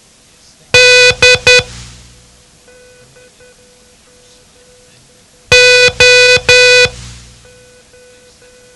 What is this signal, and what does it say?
Morse code, heard as a keyed tone, about 10 wpm, 500 Hz. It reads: D O
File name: ham radio / CodeQuick / Code Quick 07